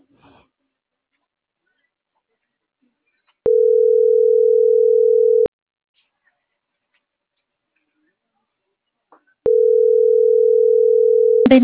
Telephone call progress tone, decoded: ringback tone

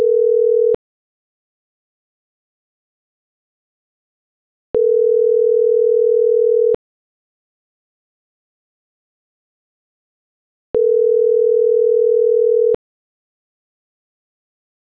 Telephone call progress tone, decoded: ringback tone

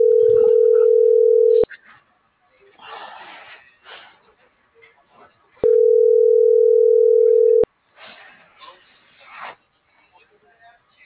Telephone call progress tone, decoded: ringback tone

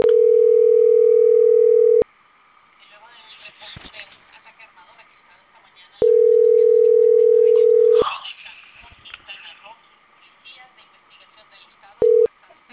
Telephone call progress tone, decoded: ringback tone

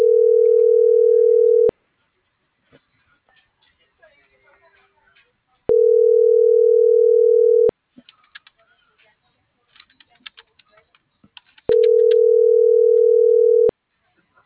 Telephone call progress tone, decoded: ringback tone

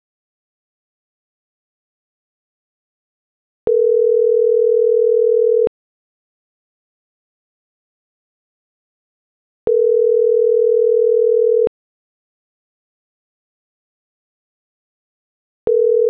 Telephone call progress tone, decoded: ringback tone

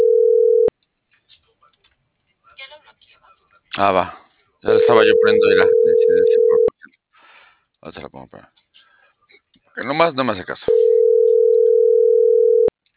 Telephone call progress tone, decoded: ringback tone

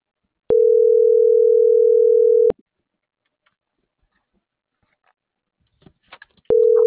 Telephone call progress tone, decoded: ringback tone

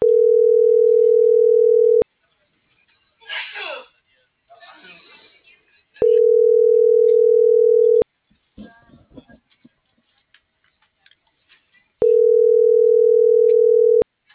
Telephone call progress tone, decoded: ringback tone

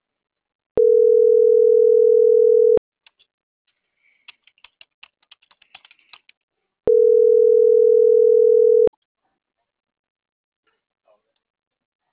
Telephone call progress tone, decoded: ringback tone

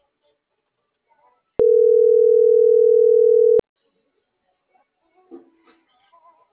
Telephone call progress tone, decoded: ringback tone

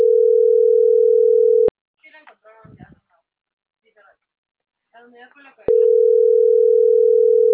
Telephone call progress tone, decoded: ringback tone

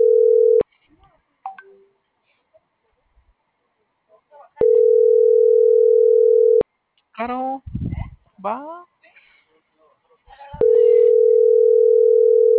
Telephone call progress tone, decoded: ringback tone

